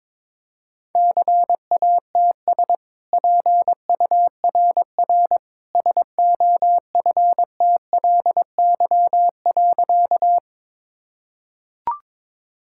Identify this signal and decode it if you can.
Morse code, heard as a keyed tone, about 22 wpm, 690 Hz. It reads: CATS PURR SOFTLY.
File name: practice / 22wpm / Easy Sentences with Hints - 22wpm